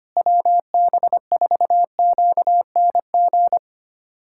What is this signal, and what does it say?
Morse code, heard as a keyed tone, about 25 wpm, 700 Hz. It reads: WB4QNG
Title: USA Callsign DE Callsign - 6 Characters - 25wpm